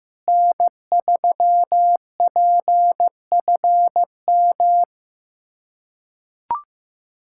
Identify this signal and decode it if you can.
Morse code, heard as a keyed tone, about 16 wpm, 700 Hz. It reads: N3PFM
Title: K1USN SST QSO Practice - 15wpm